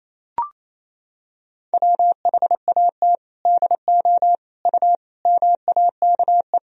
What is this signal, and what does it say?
Morse code, heard as a keyed tone, about 28 wpm, 700 Hz. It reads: WHAT DO U MAKE
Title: Sentences from Top 100 Words and Abbreviations 28wpm